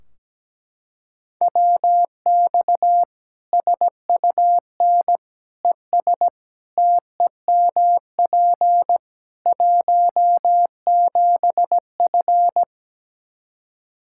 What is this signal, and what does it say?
Morse code, heard as a keyed tone, about 17 wpm, 700 Hz. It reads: WX SUN ES TEMP 17F
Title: Practice QSO - Exchange 2 - Info - 17wpm